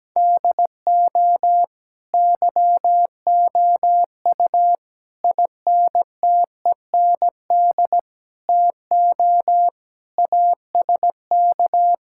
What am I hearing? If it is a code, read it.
Morse code, heard as a keyed tone, about 17 wpm, 700 Hz. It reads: DO YOU INTEND TO ASK
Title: Sentences from Top 1600 Words 17wpm